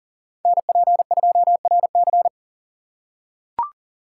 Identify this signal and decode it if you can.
Morse code, heard as a keyed tone, about 40 wpm, 700 Hz. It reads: NP2RC
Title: USA Call Signs - 4x Character Spacing - 40wpm